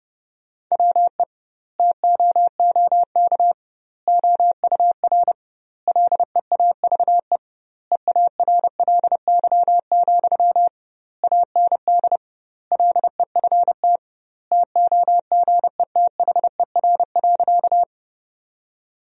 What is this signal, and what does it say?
Morse code, heard as a keyed tone, about 30 wpm, 700 Hz. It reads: WE TOOK OUR LEAVE EARLY, AND LEFT TOGETHER.
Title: 2x Word-Spacing - Sentences from Top 400 Words - 30wpm